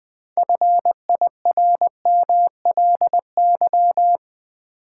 Morse code, heard as a keyed tone, about 20 wpm, 700 Hz. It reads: FIRMLY
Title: 6 Letter Words - First 3 Letters Given - 20wpm